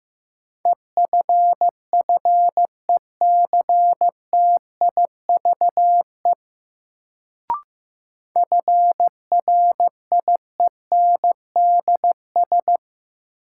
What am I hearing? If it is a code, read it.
Morse code, heard as a keyed tone, about 15 wpm, 700 Hz. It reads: EFFECTIVE FRIENDS
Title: Sentences from Top 1600 Words - New Word Drill Alt - 15wpm